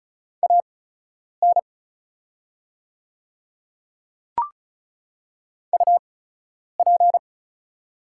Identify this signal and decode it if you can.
Morse code, heard as a keyed tone, about 36 wpm, spaced as for 10 wpm, 700 Hz. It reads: AN UP
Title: 8x Character Spacing - Top 100 Words - 35wpm